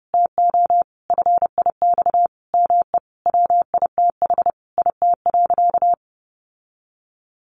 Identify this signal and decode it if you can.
Morse code, heard as a keyed tone, about 30 wpm, 700 Hz. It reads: TO FIX ME WITH IT.
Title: Sentences from Top 1400 Words 30wpm